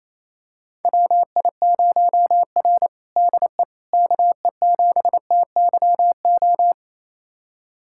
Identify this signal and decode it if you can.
Morse code, heard as a keyed tone, about 28 wpm, 700 Hz. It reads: WI0R DE KE7TYO
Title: USA Callsign DE Callsign 28wpm